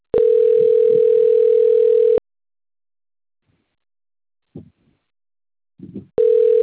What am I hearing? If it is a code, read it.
Ringback tone (North America), 440 + 480 Hz.